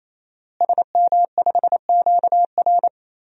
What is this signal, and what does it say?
Morse code, heard as a keyed tone, about 28 wpm, 700 Hz. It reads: SM5QR
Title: DXCC-100 Call Signs - 5 Characters - Repeated 3x - 28wpm